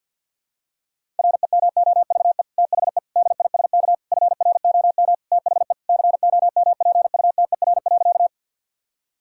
Morse code, heard as a keyed tone, about 50 wpm, 700 Hz. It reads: REMOVE THE DISK FROM THE COMPUTER.
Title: Sentences from Top 1000 Words - Review Part 6 - 50wpm